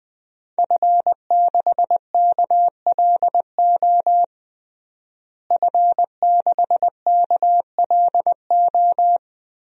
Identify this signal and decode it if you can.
Morse code, heard as a keyed tone, about 21 wpm, 700 Hz. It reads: F6KLO F6KLO
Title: DXCC-100 Call Signs - 5 Characters - Repeated 3x - 20wpm